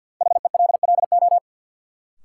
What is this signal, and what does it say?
Morse code, heard as a keyed tone, about 53 wpm, 700 Hz. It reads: HELLO
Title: Sentences from Top 1100 Words - New Word Drill - 50wpm